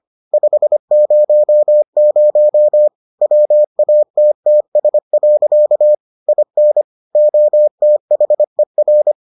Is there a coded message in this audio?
Yes: Morse code from 0.3 to 9.1 s, heard as a keyed tone, about 25 wpm, 600 Hz. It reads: 500 WATTS. IN OTHER